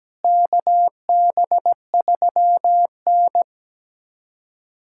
Morse code, heard as a keyed tone, about 17 wpm, 700 Hz. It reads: KB3N
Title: USA Callsign DE Callsign - 4 Characters - 17wpm